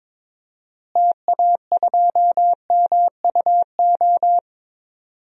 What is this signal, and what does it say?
Morse code, heard as a keyed tone, about 22 wpm, 700 Hz. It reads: TA2MUO
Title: DXCC-100 Call Signs - 6 Characters - Repeated 3x - 22wpm